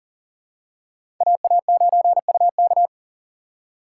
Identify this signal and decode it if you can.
Morse code, heard as a keyed tone, about 40 wpm, 700 Hz. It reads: AA9UK
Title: USA Call Signs - 3x Character Spacing - 40wpm